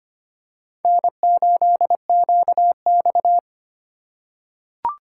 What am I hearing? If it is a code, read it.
Morse code, heard as a keyed tone, about 25 wpm, 700 Hz. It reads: N8QX
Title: USA Call Signs - 4 Characters - 2x Character Spacing - 25wpm